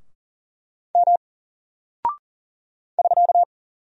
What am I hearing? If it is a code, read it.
Morse code, heard as a keyed tone, about 40 wpm, 700 Hz. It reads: M <SK>
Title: Single Character (Letter:Number:Punctuation:Prosign) Alt 40wpm